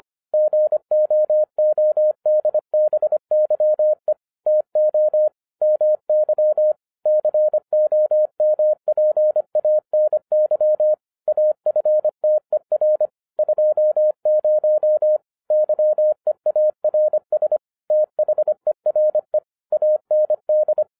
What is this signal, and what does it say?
Morse code, heard as a keyed tone, about 25 wpm, 610 Hz. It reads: GOODBYE TO MY COMPANY AFTER 20 YEARS THERE AND